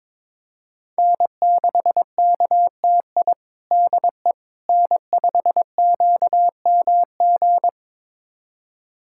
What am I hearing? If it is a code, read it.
Morse code, heard as a keyed tone, about 22 wpm, 700 Hz. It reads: N6KTI DE N5QMG
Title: USA Callsign DE Callsign - 5 Characters - 22wpm